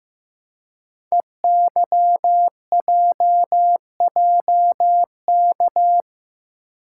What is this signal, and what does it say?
Morse code, heard as a keyed tone, about 15 wpm, 700 Hz. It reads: EYJJK
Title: Code Groups - 5 Letters - 15wpm